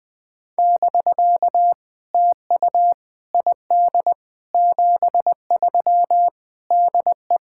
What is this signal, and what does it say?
Morse code, heard as a keyed tone, about 20 wpm, 700 Hz. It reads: <BK> TU ID 73 DE
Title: POTA QSO Practice - 20wpm